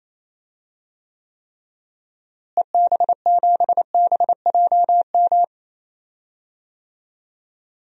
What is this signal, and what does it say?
Morse code, heard as a keyed tone, about 28 wpm, 700 Hz. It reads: EB7BJM
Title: DXCC-100 Call Signs - 6 Characters - Repeated 3x - 28wpm